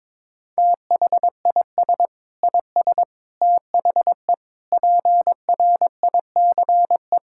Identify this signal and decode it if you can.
Morse code, heard as a keyed tone, about 21 wpm, 700 Hz. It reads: THIS IS THE PRICE
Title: Sentences from Top 1000 Words - Review Part 5 - 22wpm